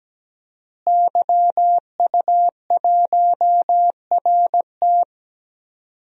Morse code, heard as a keyed tone, about 17 wpm, 700 Hz. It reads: YU1RT
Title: DXCC-100 Call Signs - 5 Characters - Repeated 3x - 17wpm